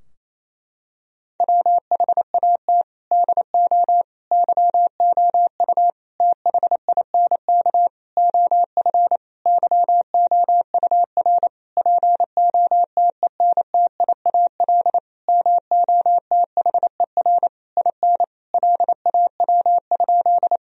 Morse code, heard as a keyed tone, about 28 wpm, 700 Hz. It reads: WHAT DO YOU THINK OF YOUR POTENTIAL MOTHER IN LAW?
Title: Sentences from Top 1000 Words - Review Part 4 - 28wpm